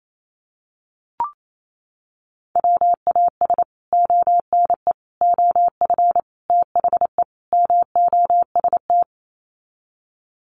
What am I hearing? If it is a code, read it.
Morse code, heard as a keyed tone, about 28 wpm, 700 Hz. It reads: WAS ONE OF THE MOST
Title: Sets of 5 Words 28wpm